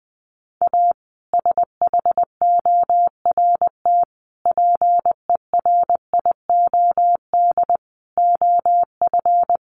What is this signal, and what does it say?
Morse code, heard as a keyed tone, about 20 wpm, 700 Hz. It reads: A SHORT PERIOD OF
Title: Sets of 5 Words Encore 20wpm